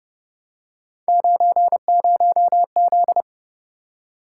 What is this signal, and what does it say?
Morse code, heard as a keyed tone, about 30 wpm, 700 Hz. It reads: 90Z
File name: Speed-Racing - Three Letter-Number Combinations - 25wpm